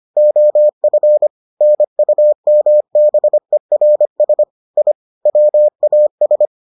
Morse code, heard as a keyed tone, about 25 wpm, 600 Hz. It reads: OF NUMBERS I WAS